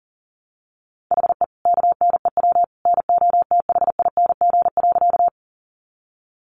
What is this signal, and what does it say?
Morse code, heard as a keyed tone, about 37 wpm, 700 Hz. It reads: HE KNEW NOTHING.
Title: Sentences from Top 500 Words 40wpm